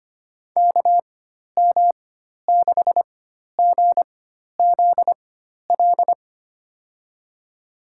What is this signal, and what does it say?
Morse code, heard as a keyed tone, about 25 wpm, spaced as for 12 wpm, 700 Hz. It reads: KM6GZL
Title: USA Call Signs - 6 Characters - 4x Character Spacing - 25wpm